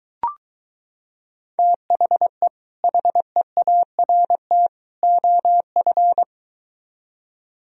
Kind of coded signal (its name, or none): Morse code